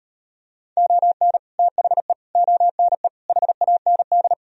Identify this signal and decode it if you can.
Morse code, heard as a keyed tone, about 38 wpm, 700 Hz. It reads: ON THE ONE HAND